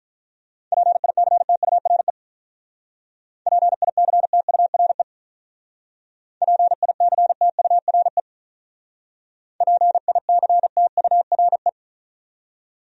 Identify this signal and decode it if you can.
Morse code, heard as a keyed tone, about 46 wpm, 700 Hz. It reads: PICTURE PICTURE PICTURE PICTURE